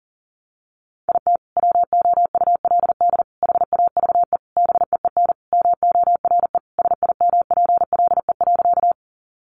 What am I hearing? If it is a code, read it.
Morse code, heard as a keyed tone, about 40 wpm, 700 Hz. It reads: IT WOULD HAVE BEEN MORE SIMPLE.
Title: Sentences from Top 400 Words 40wpm